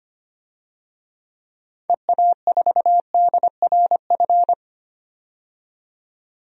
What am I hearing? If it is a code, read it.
Morse code, heard as a keyed tone, about 25 wpm, 700 Hz. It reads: EA4DRF
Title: DXCC-100 Call Signs - 6 Characters - 25wpm